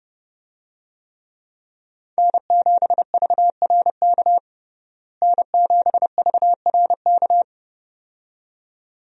Morse code, heard as a keyed tone, about 30 wpm, 700 Hz. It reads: N7VRK N7VRK
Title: USA Call Signs - 5 Characters - Repeated 2x - 30wpm